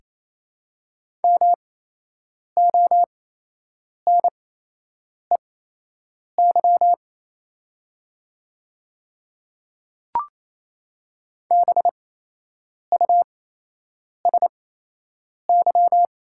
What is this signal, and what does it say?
Morse code, heard as a keyed tone, about 28 wpm, spaced as for 8 wpm, 700 Hz. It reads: MONEY BUSY
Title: 8x Character Spacing - Top 500 Words - 28wpm